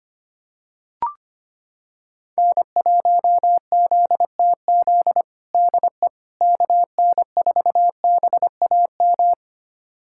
Morse code, heard as a keyed tone, about 25 wpm, 700 Hz. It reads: N1ZTZ DE KN4BAM
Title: USA Callsign DE Callsign 25wpm